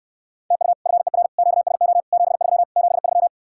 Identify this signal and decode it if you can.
Morse code, heard as a keyed tone, about 67 wpm, 700 Hz. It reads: TU FER QSO 73 73